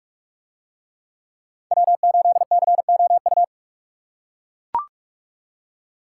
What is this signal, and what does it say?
Morse code, heard as a keyed tone, about 41 wpm, 700 Hz. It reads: W8COU